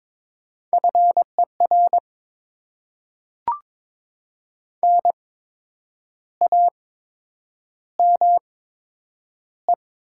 Morse code, heard as a keyed tone, about 22 wpm, spaced as for 8 wpm, 700 Hz. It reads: FER NAME
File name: Top 20 QSO Elements - 8x Character Spacing - 22wpm